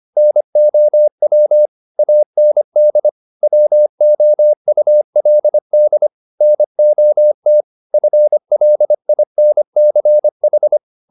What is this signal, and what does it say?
Morse code, heard as a keyed tone, about 25 wpm, 600 Hz. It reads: NOW AND WOULD NOT FLINCH